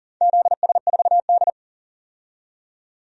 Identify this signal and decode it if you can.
Morse code, heard as a keyed tone, about 42 wpm, 700 Hz. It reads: ZS4D
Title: DXCC-100 Call Signs - 4 Characters - 40wpm